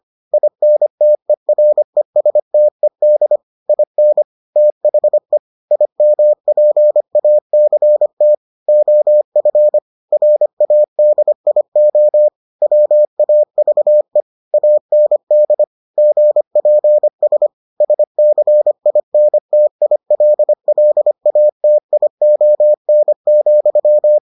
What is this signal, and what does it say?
Morse code, heard as a keyed tone, about 25 wpm, 600 Hz. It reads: INTERESTED IN THE IMPACT OF RADIO WAVE AND GPS SCINTILLATION,